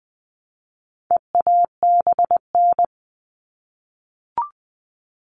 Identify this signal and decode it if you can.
Morse code, heard as a keyed tone, about 20 wpm, 700 Hz. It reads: EABN
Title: All Cut Numbers - 4-Digit 20wpm